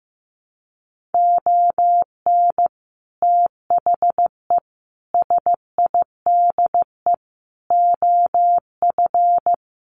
Morse code, heard as a keyed tone, about 15 wpm, 700 Hz. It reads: ON THE SIDE OF
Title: Sets of 4 Words 15wpm